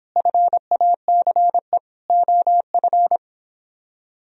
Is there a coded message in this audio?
Yes: Morse code from 0.2 to 3.2 s, heard as a keyed tone, about 31 wpm, 700 Hz. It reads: FACE OF